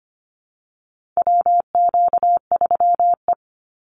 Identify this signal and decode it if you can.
Morse code, heard as a keyed tone, about 25 wpm, 700 Hz. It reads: WQ3E
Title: Call Signs Repeated 3x - 25wpm